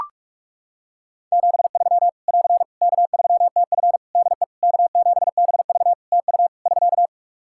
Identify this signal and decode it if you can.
Morse code, heard as a keyed tone, about 44 wpm, 700 Hz. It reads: 73 <AR> K3TF DE K7BV TU <SK>
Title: Practice QSO - Exchange 4 - Ending - 45wpm